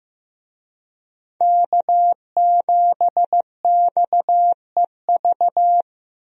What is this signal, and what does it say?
Morse code, heard as a keyed tone, about 15 wpm, 700 Hz. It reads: K7XEV
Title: USA Call Signs - 5 Characters - Repeated 2x - 15wpm